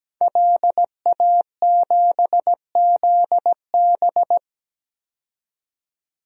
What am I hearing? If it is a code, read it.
Morse code, heard as a keyed tone, about 17 wpm, 700 Hz. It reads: LA7ZB
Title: DXCC-100 Call Signs - 5 Characters - 17wpm